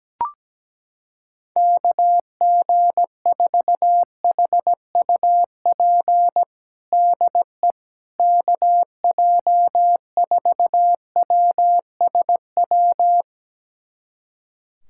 Morse code, heard as a keyed tone, about 17 wpm, 700 Hz. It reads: KG4HUP DE KJ4WSW
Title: USA Callsign DE Callsign - 6 Characters - 17wpm